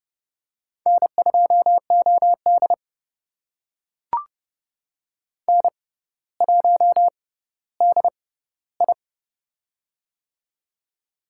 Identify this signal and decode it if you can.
Morse code, heard as a keyed tone, about 30 wpm, spaced as for 15 wpm, 700 Hz. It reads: N2OD N1DI
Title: USA Call Signs - 4 Characters - 6x Character Spacing - 30wpm